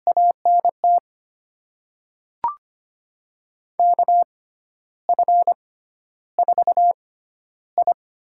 Morse code, heard as a keyed tone, about 25 wpm, spaced as for 12 wpm, 700 Hz. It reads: ANT KF4I
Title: USA Call Signs - 5 Characters - 6x Character Spacing - 25wpm